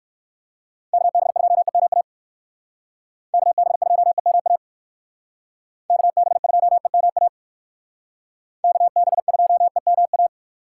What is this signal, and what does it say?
Morse code, heard as a keyed tone, about 58 wpm, 700 Hz. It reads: KB2EMA KB2EMA KB2EMA KB2EMA